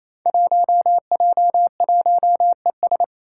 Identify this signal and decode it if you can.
Morse code, heard as a keyed tone, about 28 wpm, 700 Hz. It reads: 1J1ES
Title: Five Challenging Letter-Number Combinations 28wpm